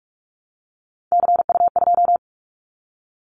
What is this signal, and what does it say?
Morse code, heard as a keyed tone, about 46 wpm, 700 Hz. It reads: CU2
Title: Callsign Prefixes for Top 100 DXCC - 45wpm